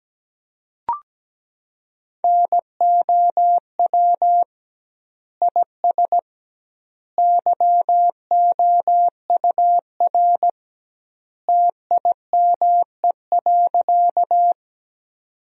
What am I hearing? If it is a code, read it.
Morse code, heard as a keyed tone, about 17 wpm, 700 Hz. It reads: NOW IS YOUR TIME.
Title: 2x Word-Spacing - Sentences from Top 100 Words - 17wpm